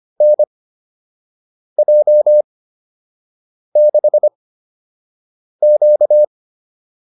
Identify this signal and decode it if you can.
Morse code, heard as a keyed tone, about 25 wpm, spaced as for 6 wpm, 600 Hz. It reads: NJ6Q